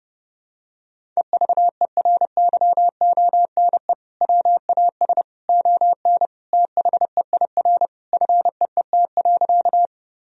Morse code, heard as a keyed tone, about 30 wpm, 700 Hz. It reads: EVERYONE WAS ON THEIR FEET.
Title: Sentences from Top 1000 Words - Review Part 1 - 30wpm